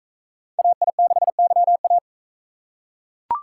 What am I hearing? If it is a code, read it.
Morse code, heard as a keyed tone, about 39 wpm, 700 Hz. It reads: AI6YA